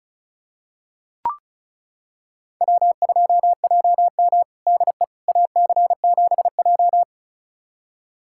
Morse code, heard as a keyed tone, about 36 wpm, 700 Hz. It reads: W2JM DE AC7J
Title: USA Callsign DE Callsign 35wpm